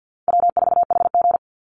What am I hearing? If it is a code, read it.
Morse code, heard as a keyed tone, about 50 wpm, 700 Hz. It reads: W4HG
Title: Call Signs Repeated 2x - 50wpm